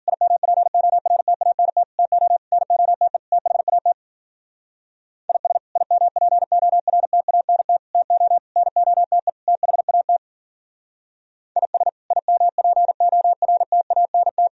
Morse code, heard as a keyed tone, about 52 wpm, 700 Hz. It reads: IMPORTANT TO NOTE THAT IS IMPORTANT TO NOTE THAT IS IMPORTANT